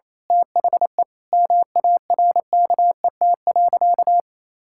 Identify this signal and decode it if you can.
Morse code, heard as a keyed tone, about 28 wpm, 700 Hz. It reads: THE MARKET.
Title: Sentences from Top 1000 Words - Review Part 5 - 28wpm